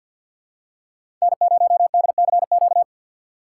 Morse code, heard as a keyed tone, about 53 wpm, 700 Hz. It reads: N0DCQ
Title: USA Call Signs - 5 Characters - 6x Character Spacing - 50wpm